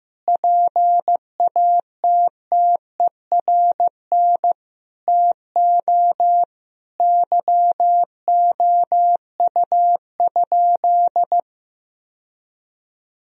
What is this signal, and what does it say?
Morse code, heard as a keyed tone, about 16 wpm, 700 Hz. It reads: PATTERN TO YOU?
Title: Sentences from Top 1000 Words - Review Part 2 - 15wpm